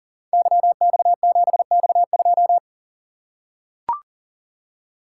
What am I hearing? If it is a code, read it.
Morse code, heard as a keyed tone, about 43 wpm, 700 Hz. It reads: YX7X2
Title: Code Groups - 5 Letter:Numbers - 40wpm